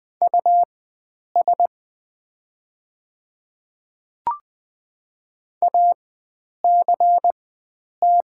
Morse code, heard as a keyed tone, about 20 wpm, spaced as for 9 wpm, 690 Hz. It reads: US ACT
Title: 4x Character Spacing - Top 300 Words - 20wpm